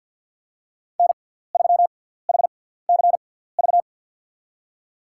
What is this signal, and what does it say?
Morse code, heard as a keyed tone, about 52 wpm, spaced as for 17 wpm, 700 Hz. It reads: N3HCV
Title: USA Call Signs - 5 Characters - 6x Character Spacing - 50wpm